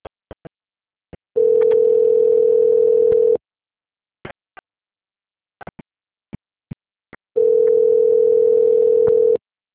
Ringback tone (North America), 440 + 480 Hz.
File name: out- - - - - .13751